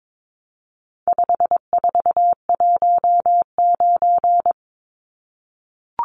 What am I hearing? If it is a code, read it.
Morse code, heard as a keyed tone, about 22 wpm, 700 Hz. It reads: 5419
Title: Numbers - 4-Digit 22wpm